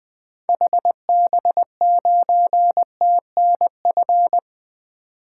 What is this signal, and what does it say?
Morse code, heard as a keyed tone, about 20 wpm, 700 Hz. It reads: HB9TNF
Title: 2023 SOTA Activator Call Signs - 20wpm